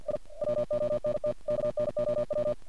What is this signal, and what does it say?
Morse code, heard as a keyed tone, about 46 wpm, 610 Hz. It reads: I WONT KNOW
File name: archived / CRSnet